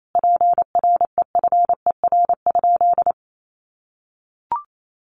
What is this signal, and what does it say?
Morse code, heard as a keyed tone, about 28 wpm, 700 Hz. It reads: PREFER?
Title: Sentences from Top 1100 Words 28wpm